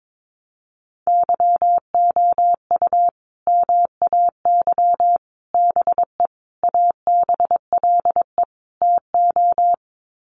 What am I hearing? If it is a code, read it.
Morse code, heard as a keyed tone, about 22 wpm, 700 Hz. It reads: YOU MAY BE ABLE TO
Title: Sets of 5 Words Encore 22wpm